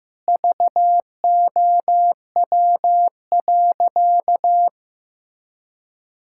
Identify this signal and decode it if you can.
Morse code, heard as a keyed tone, about 15 wpm, 700 Hz. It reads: VOW.
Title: Pangram Challenge v2 - 15wpm